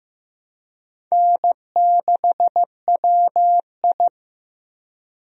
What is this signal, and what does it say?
Morse code, heard as a keyed tone, about 15 wpm, 700 Hz. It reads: N6WI
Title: USA Call Signs - 4 Characters - Repeated 3x - 15wpm